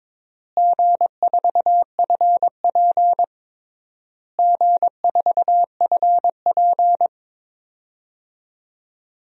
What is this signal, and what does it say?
Morse code, heard as a keyed tone, about 22 wpm, 700 Hz. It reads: G4FP G4FP
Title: DXCC-100 Call Signs - 4 Characters - Repeated 3x - 22wpm